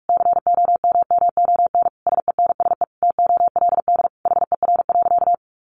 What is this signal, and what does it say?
Morse code, heard as a keyed tone, about 44 wpm, 700 Hz. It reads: COMMON SENSE TOLD HER.
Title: Sentences from Top 1200 Words 45wpm